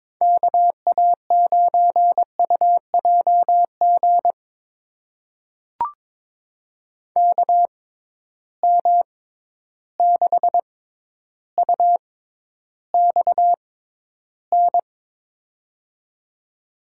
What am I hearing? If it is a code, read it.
Morse code, heard as a keyed tone, about 22 wpm, spaced as for 11 wpm, 700 Hz. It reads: KA9UJG KM6UXN
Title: USA Call Signs - 6 Characters - 6x Character Spacing - 22wpm